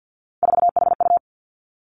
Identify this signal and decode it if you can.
Morse code, heard as a keyed tone, about 50 wpm, 700 Hz. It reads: 4HU
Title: Three Challenging Letter-Number Combinations 50wpm